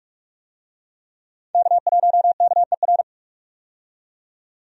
Morse code, heard as a keyed tone, about 46 wpm, 700 Hz. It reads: K1KER
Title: USA Call Signs - 5 Characters - 45wpm